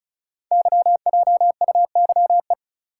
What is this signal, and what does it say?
Morse code, heard as a keyed tone, about 34 wpm, 690 Hz. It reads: YJUYE
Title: Code Groups - 5 Letters - 35wpm